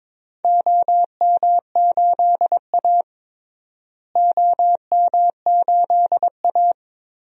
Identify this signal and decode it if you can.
Morse code, heard as a keyed tone, about 22 wpm, 700 Hz. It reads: OM8A OM8A
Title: DXCC-100 Call Signs - 4 Characters - Repeated 2x - 22wpm